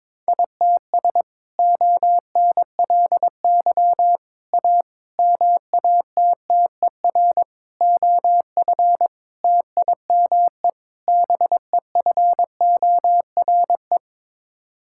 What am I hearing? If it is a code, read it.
Morse code, heard as a keyed tone, about 22 wpm, 700 Hz. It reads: ITS ONLY A MATTER OF TIME BEFORE
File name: practice / 22wpm / Sets of 7 Words - 22wpm